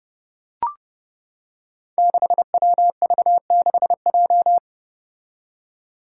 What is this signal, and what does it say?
Morse code, heard as a keyed tone, about 29 wpm, 700 Hz. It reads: 6WV6J
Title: Five Challenging Letter-Number Combinations 30wpm